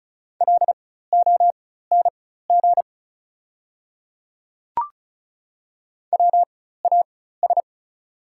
Morse code, heard as a keyed tone, about 36 wpm, spaced as for 16 wpm, 690 Hz. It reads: LONG WAS
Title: 4x Character Spacing - Top 100 Words - 35wpm